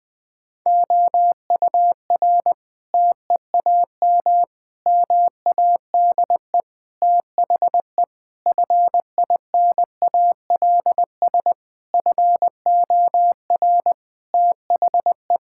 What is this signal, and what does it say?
Morse code, heard as a keyed tone, about 20 wpm, 700 Hz. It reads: OUR TEAM MADE THE FINALS FOR THE
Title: Sentences with 7-Word Phrases - 20wpm